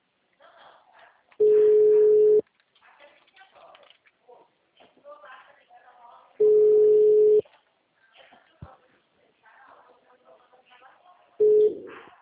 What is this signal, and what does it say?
Ringback tone (Europe, 425 Hz), with speech.